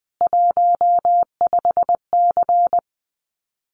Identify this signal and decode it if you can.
Morse code, heard as a keyed tone, about 20 wpm, 700 Hz. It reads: 15C
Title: Common QSO Phrases 20wpm